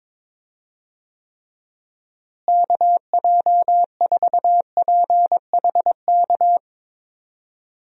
Morse code, heard as a keyed tone, about 22 wpm, 700 Hz. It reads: KJ4PHK
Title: USA Call Signs - 6 Characters - Repeated 3x - 22wpm